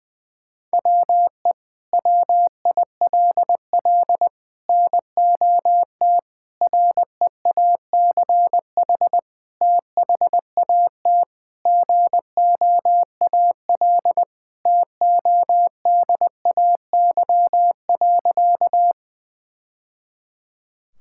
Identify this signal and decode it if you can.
Morse code, heard as a keyed tone, about 20 wpm, 700 Hz. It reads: WE WILL NOT REACH THAT GOAL TODAY.
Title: Sentences from Top 1000 Words - Review Part 5 - 20wpm